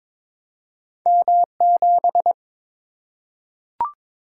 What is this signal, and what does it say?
Morse code, heard as a keyed tone, about 22 wpm, 700 Hz. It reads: M7
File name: practice / 22wpm / Two Letter-Number Combinations - 22wpm